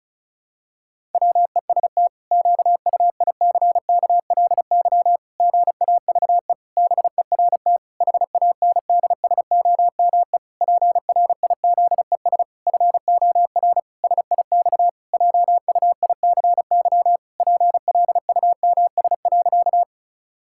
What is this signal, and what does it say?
Morse code, heard as a keyed tone, about 36 wpm, 700 Hz. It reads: WEST QUICKLY GAVE BERT HANDSOME PRIZES FOR SIX JUICY PLUMS.
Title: Pangram Challenge v2 - 35wpm